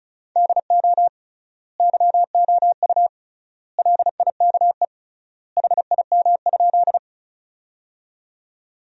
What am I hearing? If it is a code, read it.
Morse code, heard as a keyed tone, about 36 wpm, 700 Hz. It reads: DO YOU LIKE HIM?
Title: 3x Word-Spacing - Sentences from Top 100 Words - 35wpm